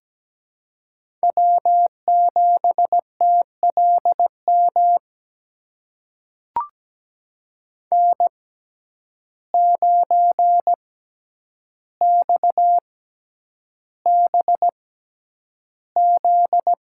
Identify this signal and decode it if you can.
Morse code, heard as a keyed tone, about 17 wpm, spaced as for 9 wpm, 700 Hz. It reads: W7TLM N9XBZ
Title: USA Call Signs - 5 Characters - 6x Character Spacing - 17wpm